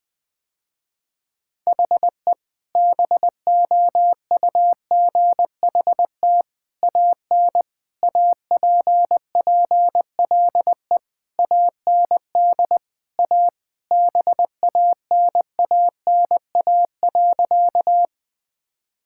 Morse code, heard as a keyed tone, about 21 wpm, 700 Hz. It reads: HE BOUGHT AN APPLE AND A BANANA.
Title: Sentences with 2-Word Phrases - Spoken Beginning - 20wpm